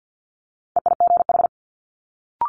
Morse code, heard as a keyed tone, about 50 wpm, 700 Hz. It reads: EIGH
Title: Multi-letter Phonemes Quick Fire 50wpm